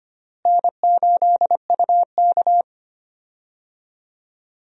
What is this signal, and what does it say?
Morse code, heard as a keyed tone, about 25 wpm, 700 Hz. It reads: N8UK
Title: USA Call Signs - 4 Characters - Repeated 3x - 25wpm